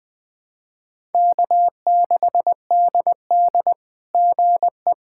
Morse code, heard as a keyed tone, about 20 wpm, 700 Hz. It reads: K6DD GE
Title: SOTA QSO Practice 20wpm